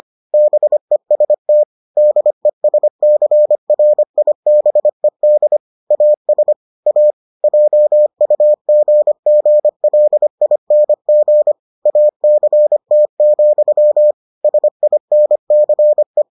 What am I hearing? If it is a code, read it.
Morse code, heard as a keyed tone, about 25 wpm, 600 Hz. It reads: BEST DESCRIBED AS A JUGGLING ACT, SINCE